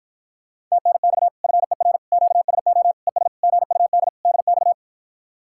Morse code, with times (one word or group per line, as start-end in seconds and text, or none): 0.7-1.3 TNX
1.4-2.0 FER
2.1-2.9 QSO
3.1-3.3 ES
3.4-4.1 GUD
4.3-4.7 DX